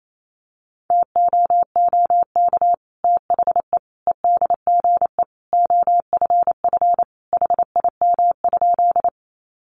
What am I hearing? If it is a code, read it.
Morse code, heard as a keyed tone, about 28 wpm, 700 Hz. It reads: TOOK THE EDGE OFF HIM?
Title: Sentences from Top 1300 Words 28wpm